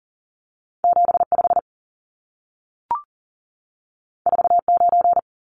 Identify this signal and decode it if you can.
Morse code, heard as a keyed tone, about 40 wpm, 700 Hz. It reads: 75 49
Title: Numbers - 2-Digit 40wpm